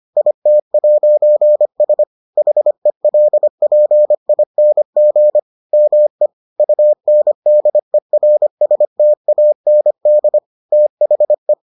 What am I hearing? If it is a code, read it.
Morse code, heard as a keyed tone, about 25 wpm, 600 Hz. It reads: IT'S HELPING ME UNDERSTAND THE